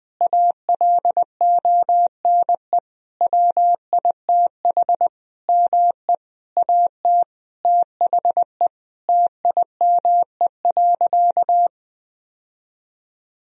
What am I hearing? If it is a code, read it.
Morse code, heard as a keyed tone, about 21 wpm, 700 Hz. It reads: ALONE WITH ME AT THE TIME.
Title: Sentences from Top 1000 Words - Review Part 4 - 20wpm